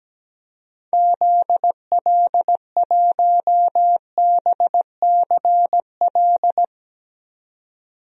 Morse code, heard as a keyed tone, about 17 wpm, 700 Hz. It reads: ZL1BCL
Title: DXCC-100 Call Signs - 6 Characters - 17wpm